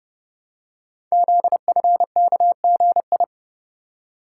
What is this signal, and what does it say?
Morse code, heard as a keyed tone, about 30 wpm, 700 Hz. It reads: ZFKGI
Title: Code Groups - 5 Letters - 30wpm